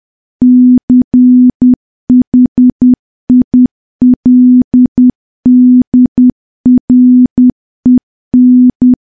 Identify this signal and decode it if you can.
Morse code, heard as a keyed tone, about 10 wpm, 250 Hz. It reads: CHILDREN
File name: morse / morse-training-10WPM